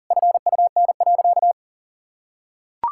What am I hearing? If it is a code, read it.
Morse code, heard as a keyed tone, about 40 wpm, 700 Hz. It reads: FUN.
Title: Sentences from Top 1600 Words 40wpm